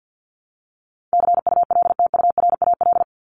Morse code, heard as a keyed tone, about 50 wpm, 700 Hz. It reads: CULTURAL
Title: Top 1251-1500 Words 50wpm